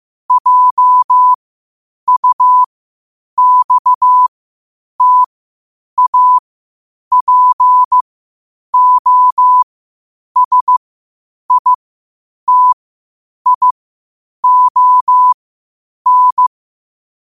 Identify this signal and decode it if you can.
Morse code, heard as a keyed tone, about 15 wpm, spaced as for 8 wpm, 1000 Hz. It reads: JUXTAPOSITION
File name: qotdfiles / 8wpm